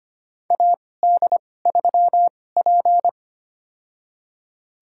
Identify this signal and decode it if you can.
Morse code, heard as a keyed tone, about 25 wpm, spaced as for 18 wpm, 700 Hz. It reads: AD3P
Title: USA Call Signs - 2x Character Spacing - 25wpm